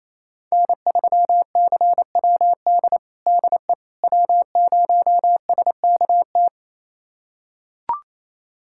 Morse code, heard as a keyed tone, about 28 wpm, 700 Hz. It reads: N3CWD DE W0SKT
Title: USA Callsign DE Callsign - 5 Characters - 28wpm